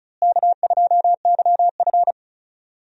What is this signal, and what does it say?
Morse code, heard as a keyed tone, about 36 wpm, 700 Hz. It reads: K2YF
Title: USA Call Signs - 4 Characters - Repeated 2x - 35wpm